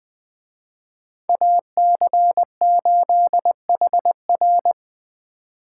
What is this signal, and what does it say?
Morse code, heard as a keyed tone, about 24 wpm, 700 Hz. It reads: AC8HR
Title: Speed Racing - USA 2x2 Call Signs 20wpm